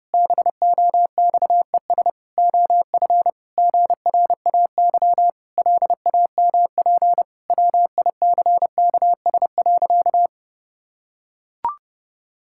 Morse code, heard as a keyed tone, about 30 wpm, 700 Hz. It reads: BOXES OF GRAY LAMP WICKS.
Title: Pangram Challenge v2 - 30wpm